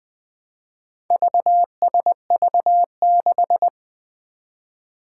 Morse code, heard as a keyed tone, about 20 wpm, 700 Hz. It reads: VSV6
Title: Four Challenging Letter-Number Combinations 20wpm